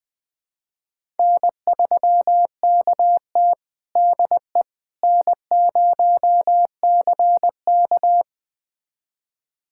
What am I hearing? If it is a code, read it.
Morse code, heard as a keyed tone, about 20 wpm, 700 Hz. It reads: N3KT DE N0CK
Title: USA Callsign DE Callsign - 4 Characters - 20wpm